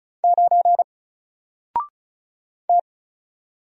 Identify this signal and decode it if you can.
Morse code, heard as a keyed tone, about 36 wpm, 700 Hz. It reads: 9 T
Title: Single Character (Letter:Number:Punctuation:Prosign) 35wpm